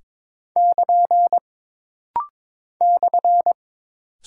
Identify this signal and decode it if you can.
Morse code, heard as a keyed tone, about 22 wpm, 700 Hz. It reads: <KN> /
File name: practice / 22wpm / Single Punctuation-Prosign - Rapid-Fire - 22wpm